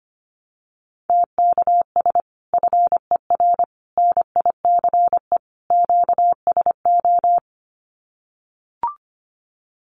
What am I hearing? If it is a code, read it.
Morse code, heard as a keyed tone, about 25 wpm, 700 Hz. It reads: TKS FER NICE QSO
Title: Common QSO Phrases 25wpm